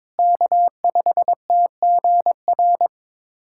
Morse code, heard as a keyed tone, about 22 wpm, 700 Hz. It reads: K5TGR
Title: USA Callsign DE Callsign - 5 Characters - 22wpm